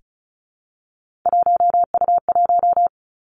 Morse code, heard as a keyed tone, about 36 wpm, 700 Hz. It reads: 1U1
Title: Three Challenging Letter-Number Combinations 35wpm